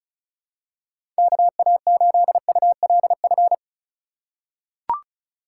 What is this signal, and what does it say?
Morse code, heard as a keyed tone, about 36 wpm, 700 Hz. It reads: KA8ULF
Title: USA Call Signs - 6 Characters - 3x Character Spacing - 35wpm